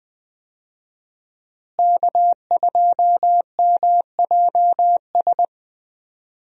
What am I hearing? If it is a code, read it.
Morse code, heard as a keyed tone, about 21 wpm, 700 Hz. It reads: K2MJS
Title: USA Call Signs - 5 Characters - Repeated 3x - 20wpm